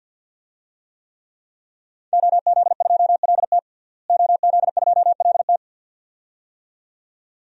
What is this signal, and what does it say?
Morse code, heard as a keyed tone, about 50 wpm, 700 Hz. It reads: OZ2LT OZ2LT
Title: DXCC-100 Call Signs - 5 Characters - Repeated 2x - 50wpm